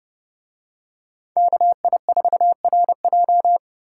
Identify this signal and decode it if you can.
Morse code, heard as a keyed tone, about 28 wpm, 700 Hz. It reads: KI4RJ